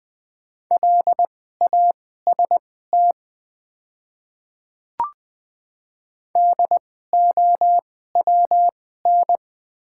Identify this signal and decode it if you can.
Morse code, heard as a keyed tone, about 20 wpm, spaced as for 14 wpm, 690 Hz. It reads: LAST DOWN
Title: 2x Character Spacing - Top 500 Words - 20wpm